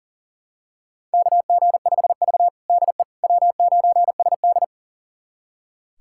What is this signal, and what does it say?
Morse code, heard as a keyed tone, about 40 wpm, 700 Hz. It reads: KG5V DE W9SD
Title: USA Callsign DE Callsign - 4 Characters - 40wpm